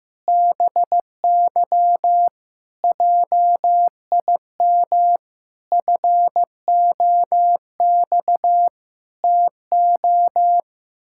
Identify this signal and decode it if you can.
Morse code, heard as a keyed tone, about 15 wpm, 700 Hz. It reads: BY JIM FOX TO
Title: Pangram Challenge v2 - 15wpm